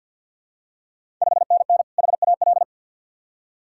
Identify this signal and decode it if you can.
Morse code, heard as a keyed tone, about 52 wpm, 700 Hz. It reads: 5NN HAL
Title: Ontario QSO Party Exchange 50wpm